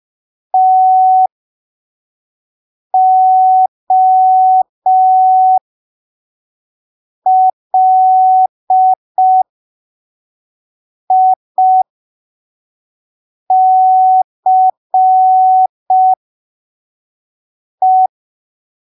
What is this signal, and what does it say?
Morse code, heard as a keyed tone, about 5 wpm, spaced as for 3 wpm, 750 Hz. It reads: TOLICE
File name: arquivos / texto / texto 15 05ppm